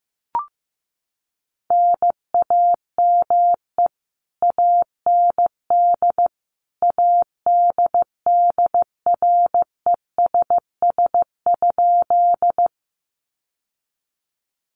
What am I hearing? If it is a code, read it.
Morse code, heard as a keyed tone, about 15 wpm, 700 Hz. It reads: NAME AND ADDRESS?
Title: Sentences from Top 1000 Words 15wpm